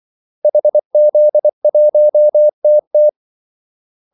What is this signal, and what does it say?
Morse code, heard as a keyed tone, about 24 wpm, 600 Hz. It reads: HZ1TT
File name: Callsigns-24wpm